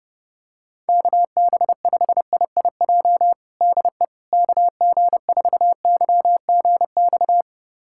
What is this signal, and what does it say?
Morse code, heard as a keyed tone, about 32 wpm, 700 Hz. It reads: KB5IIJ DE KG4YGX
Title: USA Callsign DE Callsign - 6 Characters - 30wpm